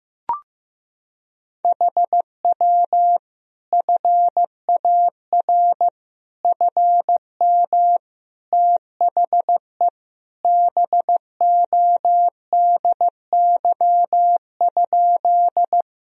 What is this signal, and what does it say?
Morse code, heard as a keyed tone, about 16 wpm, 700 Hz. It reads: HW FAR FM THE BODY?
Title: Sentences from Top 400 Words and Abbreviations 15wpm